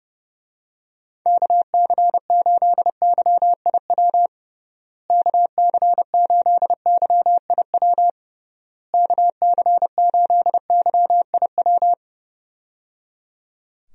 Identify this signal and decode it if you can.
Morse code, heard as a keyed tone, about 29 wpm, 700 Hz. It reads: KC8YIW KC8YIW KC8YIW
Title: USA Call Signs - 6 Characters - Repeated 3x - 30wpm